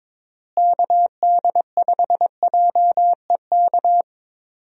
Morse code, heard as a keyed tone, about 22 wpm, 700 Hz. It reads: KD5JEK
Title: USA Call Signs - 6 Characters - Repeated 3x - 22wpm